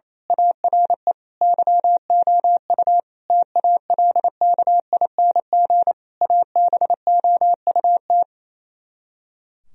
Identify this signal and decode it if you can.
Morse code, heard as a keyed tone, about 26 wpm, 700 Hz. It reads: ARE YOU TALKING ABOUT
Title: Speed-Racing - Sets of 5 Words 28wpm